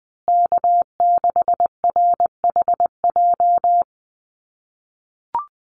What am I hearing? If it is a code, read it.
Morse code, heard as a keyed tone, about 20 wpm, 700 Hz. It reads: K6RHJ
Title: USA Call Signs Repeated 3x 20wpm